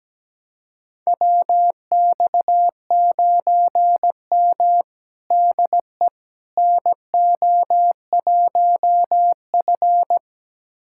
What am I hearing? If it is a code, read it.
Morse code, heard as a keyed tone, about 17 wpm, 700 Hz. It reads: WX9M DE NO1F
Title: USA Callsign DE Callsign - 4 Characters - 17wpm